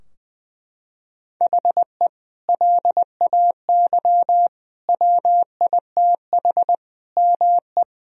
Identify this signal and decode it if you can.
Morse code, heard as a keyed tone, about 20 wpm, 700 Hz. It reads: HE LAY WITH ME
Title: Sentences from Top 1000 Words - Review Part 2 - 20wpm